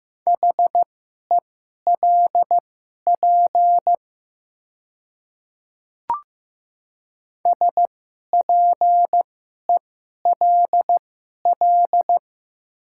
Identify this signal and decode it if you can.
Morse code, heard as a keyed tone, about 15 wpm, spaced as for 11 wpm, 700 Hz. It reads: HELP SPELL
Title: 2x Character Spacing - Top 200 Words - 15wpm